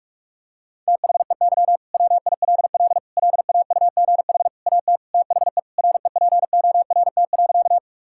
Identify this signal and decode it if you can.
Morse code, heard as a keyed tone, about 46 wpm, 690 Hz. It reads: THEY WILL LAUGH AT THE REPORT.
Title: Sentences from Top 1000 Words - Review Part 6 - 45wpm